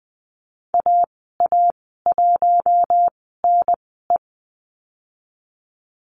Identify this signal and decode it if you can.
Morse code, heard as a keyed tone, about 20 wpm, spaced as for 15 wpm, 700 Hz. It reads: AA1NE
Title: 082 Lesson 14g - Number 2 - Call Signs 20wpm